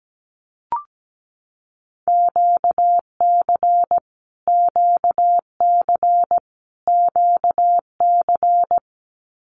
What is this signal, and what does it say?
Morse code, heard as a keyed tone, about 17 wpm, 700 Hz. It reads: QC QC QC
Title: Canadian Provinces and Territories - Repeated 3x - 17wpm